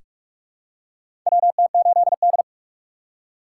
Morse code, heard as a keyed tone, about 46 wpm, 700 Hz. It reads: WT8D
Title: USA Call Signs - 4x Character Spacing - 45wpm